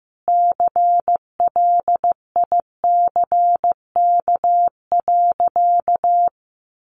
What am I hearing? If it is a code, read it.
Morse code, heard as a keyed tone, about 14 wpm, 700 Hz. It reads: CLICK.
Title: Sentences from Top 1100 Words 15wpm